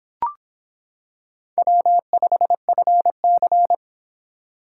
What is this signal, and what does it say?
Morse code, heard as a keyed tone, about 26 wpm, 700 Hz. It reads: W5FC